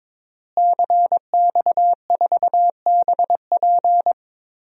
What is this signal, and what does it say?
Morse code, heard as a keyed tone, about 22 wpm, 700 Hz. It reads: CX4BP
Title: DXCC-100 Call Signs - 5 Characters - Repeated 3x - 22wpm